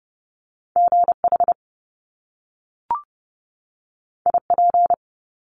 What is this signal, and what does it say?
Morse code, heard as a keyed tone, about 30 wpm, 700 Hz. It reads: GH IP
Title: Top Two Letter Combinations in Words 30wpm